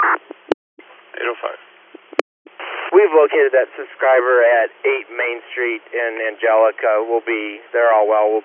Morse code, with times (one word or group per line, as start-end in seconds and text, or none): none